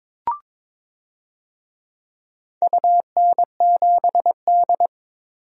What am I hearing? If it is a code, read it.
Morse code, heard as a keyed tone, about 22 wpm, 700 Hz. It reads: UN7D
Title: DXCC-100 Call Signs - 4 Characters - 22wpm